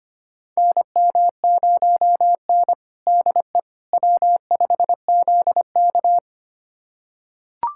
Morse code, heard as a keyed tone, about 25 wpm, 700 Hz. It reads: NM0N DE W5ZK
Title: USA Callsign DE Callsign - 4 Characters - 25wpm